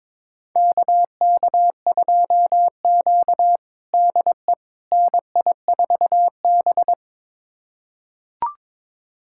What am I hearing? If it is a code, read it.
Morse code, heard as a keyed tone, about 22 wpm, 700 Hz. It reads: KK2Q DE NI4B
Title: USA Callsign DE Callsign - 4 Characters - 22wpm